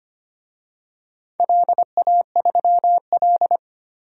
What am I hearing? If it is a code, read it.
Morse code, heard as a keyed tone, about 25 wpm, 700 Hz. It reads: LA3L
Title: DXCC-100 Call Signs - 4 Characters - 25wpm